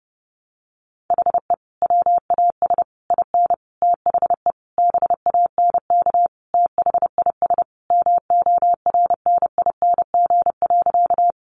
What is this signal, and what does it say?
Morse code, heard as a keyed tone, about 30 wpm, 700 Hz. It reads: HE WAS IN THE BANK THIS MORNING.
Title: Sentences from Top 900 Words 30wpm